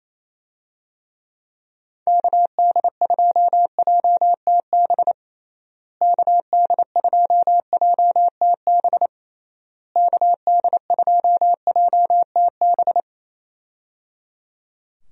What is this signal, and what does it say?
Morse code, heard as a keyed tone, about 27 wpm, 700 Hz. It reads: KD2JTB KD2JTB KD2JTB
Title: USA Call Signs - 6 Characters - Repeated 3x - 28wpm